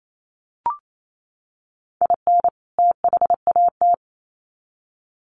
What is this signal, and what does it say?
Morse code, heard as a keyed tone, about 28 wpm, 700 Hz. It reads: IN THAT
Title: Sets of 2 Words 28wpm